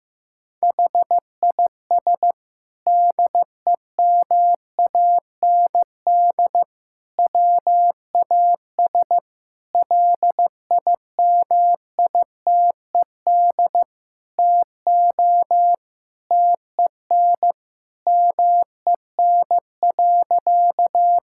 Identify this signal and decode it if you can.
Morse code, heard as a keyed tone, about 15 wpm, 700 Hz. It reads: HIS DEMAND WAS LIMITED TO TEN MEN.
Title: Sentences from Top 1000 Words - Review Part 6 - 15wpm